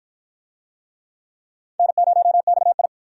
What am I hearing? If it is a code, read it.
Morse code, heard as a keyed tone, about 51 wpm, 700 Hz. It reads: N0XI